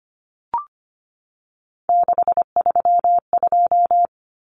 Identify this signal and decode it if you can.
Morse code, heard as a keyed tone, about 25 wpm, 700 Hz. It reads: 632
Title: Numbers - 3-Digit 25wpm